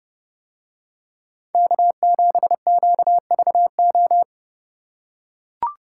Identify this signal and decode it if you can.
Morse code, heard as a keyed tone, about 29 wpm, 700 Hz. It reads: K7QVO